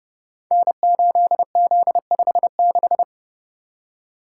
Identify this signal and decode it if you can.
Morse code, heard as a keyed tone, about 30 wpm, 700 Hz. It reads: N8Z56
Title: Code Groups - 5 Letter:Numbers - 30wpm